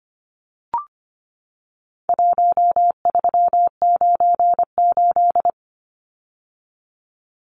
Morse code, heard as a keyed tone, about 25 wpm, 700 Hz. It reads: 1398
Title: Numbers - Contest Style - 4-Digit 25wpm